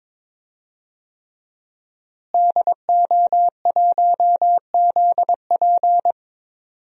Morse code, heard as a keyed tone, about 22 wpm, 700 Hz. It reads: DO1ZP
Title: DXCC-100 Call Signs - 5 Characters - Repeated 3x - 22wpm